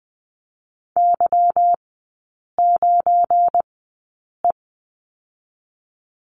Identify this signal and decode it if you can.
Morse code, heard as a keyed tone, about 20 wpm, spaced as for 8 wpm, 700 Hz. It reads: Y9E
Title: 218 Lesson 35e - Number 0 - Three Characters 20wpm